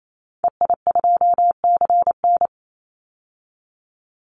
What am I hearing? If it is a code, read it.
Morse code, heard as a keyed tone, about 28 wpm, 700 Hz. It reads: EI2CN
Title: Call Signs - 28wpm